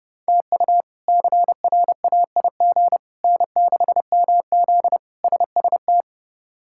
Morse code, heard as a keyed tone, about 30 wpm, 700 Hz. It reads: TU CRAIG N6MZ SST
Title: K1USN SST QSO Practice - 30wpm